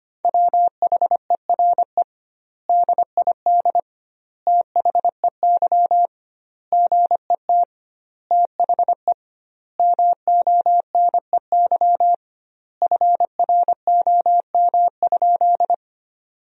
Morse code, heard as a keyed tone, about 25 wpm, 700 Hz. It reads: WHERE DID THEY GET THE MONEY FROM?
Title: 2x Word-Spacing - Sentences from Top 400 Words - 25wpm